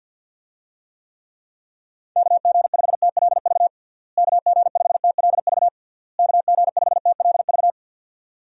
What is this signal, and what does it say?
Morse code, heard as a keyed tone, about 49 wpm, 700 Hz. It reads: KG5TLV KG5TLV KG5TLV
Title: USA Call Signs - 6 Characters - Repeated 3x - 50wpm